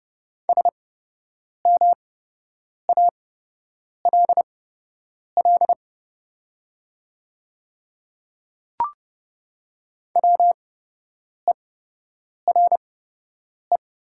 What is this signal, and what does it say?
Morse code, heard as a keyed tone, about 30 wpm, spaced as for 8 wpm, 700 Hz. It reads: SMALL WERE
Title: 8x Character Spacing - Top 300 Words - 30wpm